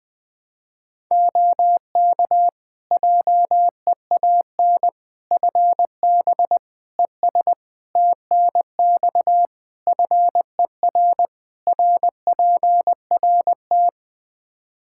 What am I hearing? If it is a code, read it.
Morse code, heard as a keyed tone, about 21 wpm, 700 Hz. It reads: OK JEAN FB ES TNX FER RPRT
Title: Practice QSO - Exchange 2 - Info - 20wpm